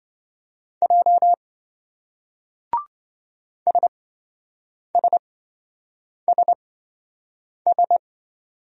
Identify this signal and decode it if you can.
Morse code, heard as a keyed tone, about 27 wpm, spaced as for 7 wpm, 700 Hz. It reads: J SSSS